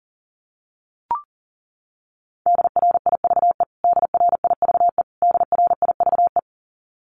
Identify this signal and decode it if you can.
Morse code, heard as a keyed tone, about 40 wpm, 700 Hz. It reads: DRIVE DRIVE DRIVE
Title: Top 500 Words Repeated 40wpm